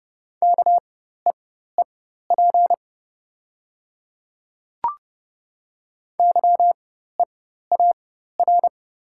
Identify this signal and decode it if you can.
Morse code, heard as a keyed tone, about 30 wpm, spaced as for 14 wpm, 690 Hz. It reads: KEEP YEAR
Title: 4x Character Spacing - Top 300 Words - 30wpm